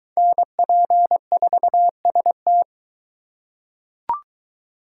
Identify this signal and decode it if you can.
Morse code, heard as a keyed tone, about 22 wpm, 700 Hz. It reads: NP4ST